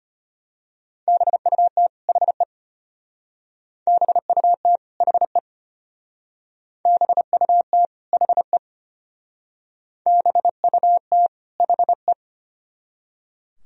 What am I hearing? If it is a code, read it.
Morse code, heard as a keyed tone, about 34 wpm, 700 Hz. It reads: BUT HE BUT HE BUT HE BUT HE